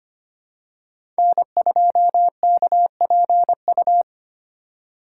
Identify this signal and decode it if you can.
Morse code, heard as a keyed tone, about 25 wpm, 700 Hz. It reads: N2KPU
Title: USA Call Signs - 5 Characters - 2x Character Spacing - 25wpm